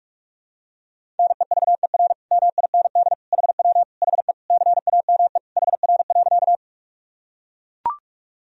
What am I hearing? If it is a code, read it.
Morse code, heard as a keyed tone, about 42 wpm, 700 Hz. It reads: NEVER MIND HW HE CAME HR.
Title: Sentences from Top 500 Words and Abbreviations 45wpm